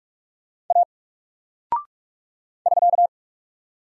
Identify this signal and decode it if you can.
Morse code, heard as a keyed tone, about 46 wpm, 700 Hz. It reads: A <SK>
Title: Single Character (Letter:Number:Punctuation:Prosign) 45wpm